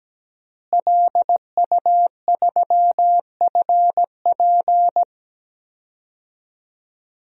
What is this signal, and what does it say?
Morse code, heard as a keyed tone, about 17 wpm, 700 Hz. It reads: LU3FP
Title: DXCC-100 Call Signs - 5 Characters - Repeated 2x - 17wpm